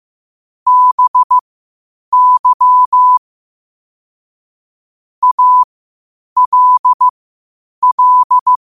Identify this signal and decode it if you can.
Morse code, heard as a keyed tone, about 15 wpm, spaced as for 8 wpm, 1000 Hz. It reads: BY ALL
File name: qotdfiles / 8wpm